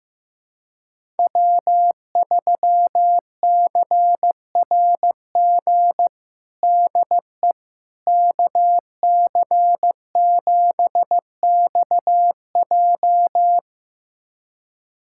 Morse code, heard as a keyed tone, about 15 wpm, 700 Hz. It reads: W3CRG DE KC7XJ
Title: USA Callsign DE Callsign - 5 Characters - 15wpm